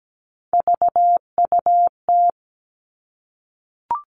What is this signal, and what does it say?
Morse code, heard as a keyed tone, about 17 wpm, 700 Hz. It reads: VUT
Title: Three Challenging Letter-Number Combinations 17wpm